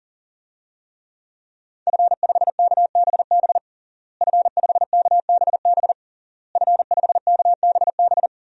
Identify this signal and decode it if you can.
Morse code, heard as a keyed tone, about 37 wpm, 700 Hz. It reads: F5KBB F5KBB F5KBB
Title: DXCC-100 Call Signs - 5 Characters - Repeated 3x - 40wpm